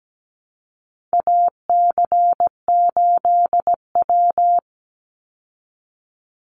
Morse code, heard as a keyed tone, about 17 wpm, 700 Hz. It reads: AC8W
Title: Call Signs Repeated 3x - 17wpm